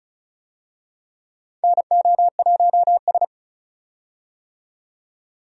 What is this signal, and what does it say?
Morse code, heard as a keyed tone, about 36 wpm, 700 Hz. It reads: NO1S
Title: USA Call Signs - 4 Characters - 35wpm